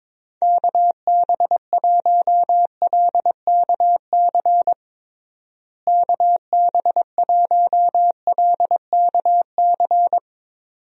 Morse code, heard as a keyed tone, about 21 wpm, 700 Hz. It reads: KB1LKC KB1LKC
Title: USA Call Signs - 6 Characters - Repeated 2x - 22wpm